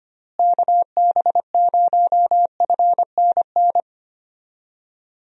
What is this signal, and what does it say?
Morse code, heard as a keyed tone, about 25 wpm, 700 Hz. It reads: KB0FNN
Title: USA Call Signs - 6 Characters - Repeated 2x - 25wpm